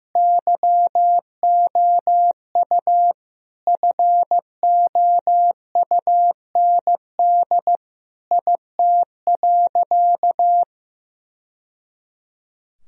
Morse code, heard as a keyed tone, about 16 wpm, 700 Hz. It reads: YOU FOUND IT.
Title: Sentences from Top 1000 Words - Review Part 4 - 15wpm